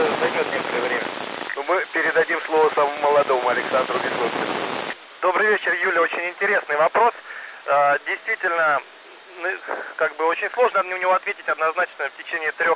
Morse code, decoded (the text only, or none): none